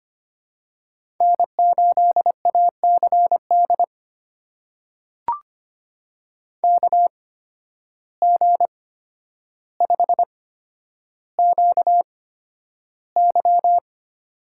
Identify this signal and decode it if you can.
Morse code, heard as a keyed tone, about 25 wpm, spaced as for 11 wpm, 700 Hz. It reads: N8ACD KG5QY
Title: USA Call Signs - 5 Characters - 8x Character Spacing - 25wpm